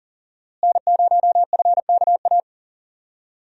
Morse code, heard as a keyed tone, about 38 wpm, 700 Hz. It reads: N0FKA
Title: USA Call Signs - 5 Characters - Repeated 3x - 40wpm